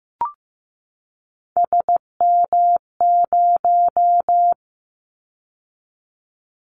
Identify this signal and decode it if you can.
Morse code, heard as a keyed tone, about 15 wpm, 700 Hz. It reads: SM0
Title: Callsign Prefixes for Top 25 DXCC - 15wpm